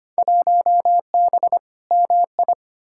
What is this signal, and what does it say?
Morse code, heard as a keyed tone, about 25 wpm, 700 Hz. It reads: 1B MI
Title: Field Day Exchanges - 25wpm